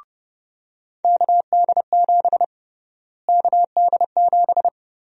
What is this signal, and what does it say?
Morse code, heard as a keyed tone, about 30 wpm, 700 Hz. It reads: KD7 KD7
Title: USA Call Sign Prefixes - Repeated 2x - 30wpm